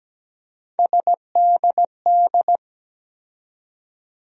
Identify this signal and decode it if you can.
Morse code, heard as a keyed tone, about 17 wpm, 700 Hz. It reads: SDD
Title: USA Call Sign Postfixes - 3 Letters 17wpm